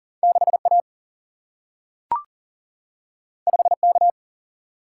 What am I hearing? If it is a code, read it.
Morse code, heard as a keyed tone, about 39 wpm, 700 Hz. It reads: 6 A 5 K
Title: Single-Letter Single-Number 40wpm